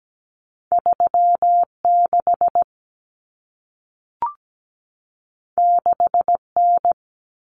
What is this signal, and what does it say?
Morse code, heard as a keyed tone, about 17 wpm, 700 Hz. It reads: 36 6N
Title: Common Cut Numbers - 2-Digit 17wpm